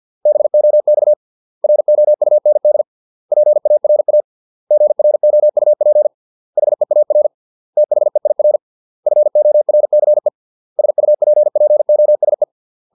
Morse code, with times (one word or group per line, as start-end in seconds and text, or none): 0.3-1.1 BOX
1.6-2.8 ROUND
3.3-4.2 PARA
4.7-6.1 GROUP
6.6-7.3 HEAR
7.8-8.6 THEIR
9.1-10.3 FORCE
10.8-12.4 SUPPOSE